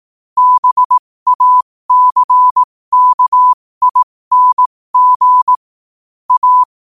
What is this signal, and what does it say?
Morse code, heard as a keyed tone, about 18 wpm, spaced as for 15 wpm, 1000 Hz. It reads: BACKING A
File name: qotdfiles / 15wpm